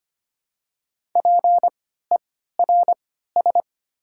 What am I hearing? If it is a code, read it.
Morse code, heard as a keyed tone, about 25 wpm, spaced as for 14 wpm, 690 Hz. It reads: PERS
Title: 3x Character Spacing - Top 500 Words - 25wpm